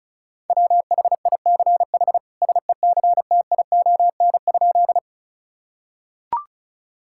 Morse code, heard as a keyed tone, about 36 wpm, 700 Hz. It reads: WHICH SECTION?
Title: Sentences from Top 1000 Words - Review Part 5 - 35wpm